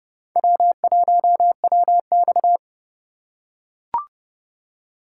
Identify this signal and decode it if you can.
Morse code, heard as a keyed tone, about 30 wpm, 700 Hz. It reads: W1WX
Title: USA Call Signs - 4 Characters - 4x Character Spacing - 30wpm